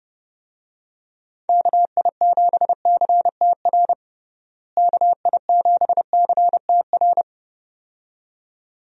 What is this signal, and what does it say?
Morse code, heard as a keyed tone, about 32 wpm, 700 Hz. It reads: KI7CTR KI7CTR
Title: USA Call Signs - 6 Characters - Repeated 2x - 30wpm